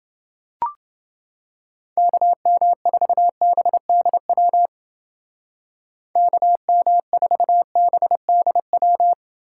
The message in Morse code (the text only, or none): KM4BDW KM4BDW